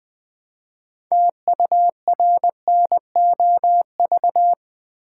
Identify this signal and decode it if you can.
Morse code, heard as a keyed tone, about 20 wpm, 700 Hz. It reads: TURNOV
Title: Build A Word - 3 Letters At A Time - 8-Letter Words - 20wpm